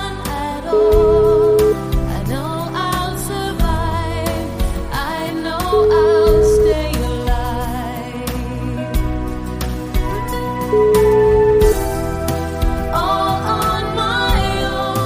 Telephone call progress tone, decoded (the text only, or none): ringback tone